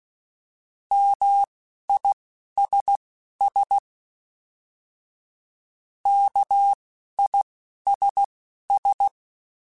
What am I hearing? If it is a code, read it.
Morse code, heard as a keyed tone, about 16 wpm, spaced as for 11 wpm, 800 Hz. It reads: MISS KISS
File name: General / General-mp3s / ISS @10